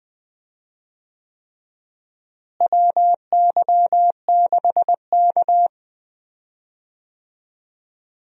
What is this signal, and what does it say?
Morse code, heard as a keyed tone, about 20 wpm, 700 Hz. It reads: WY6K
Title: USA Call Signs - 4 Characters - Repeated 3x - 20wpm